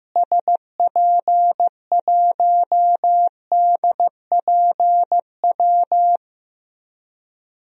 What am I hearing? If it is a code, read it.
Morse code, heard as a keyed tone, about 15 wpm, 700 Hz. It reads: SP1DPW
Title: DXCC-100 Call Signs - 6 Characters - Repeated 2x - 15wpm